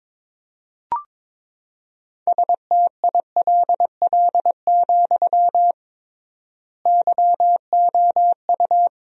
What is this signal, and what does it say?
Morse code, heard as a keyed tone, about 22 wpm, 700 Hz. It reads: STILL, YOU
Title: 3x Word-Spacing - Sentences from Top 300 Words - 22wpm